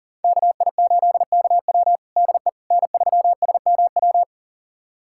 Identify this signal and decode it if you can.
Morse code, heard as a keyed tone, about 40 wpm, 700 Hz. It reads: KI8KW DE N3SMW
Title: USA Callsign DE Callsign - 5 Characters - 40wpm